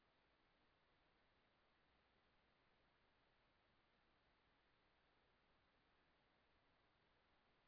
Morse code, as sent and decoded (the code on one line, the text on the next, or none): none